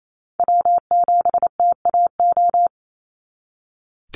Morse code, heard as a keyed tone, about 28 wpm, 700 Hz. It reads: W7TAO
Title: NA SOTA Chaser Callsigns 28wpm